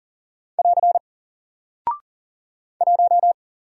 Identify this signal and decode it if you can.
Morse code, heard as a keyed tone, about 41 wpm, 700 Hz. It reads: <AR> 1
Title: Single Character (Letter:Number:Punctuation:Prosign) Alt 40wpm